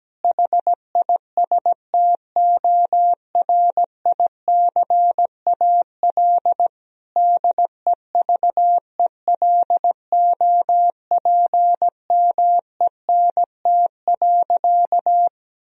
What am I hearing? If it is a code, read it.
Morse code, heard as a keyed tone, about 17 wpm, 700 Hz. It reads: HISTORICAL DEVELOPMENT.
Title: Sentences from Top 1000 Words - Review Part 4 - 17wpm